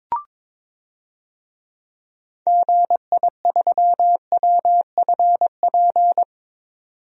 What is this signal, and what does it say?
Morse code, heard as a keyed tone, about 22 wpm, 700 Hz. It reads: GI3WFP
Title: DXCC-100 Call Signs - 6 Characters - 22wpm